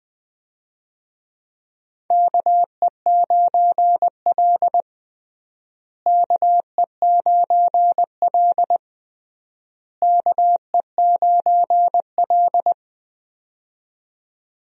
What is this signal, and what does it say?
Morse code, heard as a keyed tone, about 20 wpm, 700 Hz. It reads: KE9L KE9L KE9L
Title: USA Call Signs - 4 Characters - Repeated 3x - 20wpm